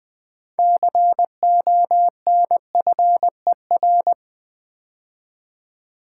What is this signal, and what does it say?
Morse code, heard as a keyed tone, about 20 wpm, 700 Hz. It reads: CONFER
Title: 6 Letter Words - First 3 Letters Given - 20wpm